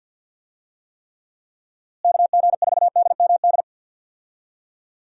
Morse code, heard as a keyed tone, about 49 wpm, 700 Hz. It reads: KG4DMD
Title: USA Call Signs - 6 Characters - Repeated 3x - 50wpm